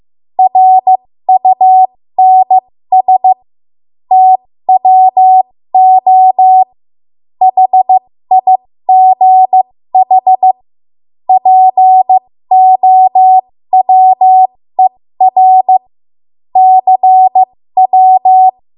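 Morse code, heard as a keyed tone, about 15 wpm, spaced as for 13 wpm, 750 Hz. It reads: RUNS TWO HIGH POWER CW
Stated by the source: 13 wpm Morse code practice archive transmission.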